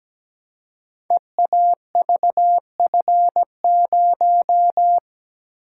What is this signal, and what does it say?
Morse code, heard as a keyed tone, about 17 wpm, 700 Hz. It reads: EAVF0
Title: Code Groups - 5 Letter:Numbers - 17wpm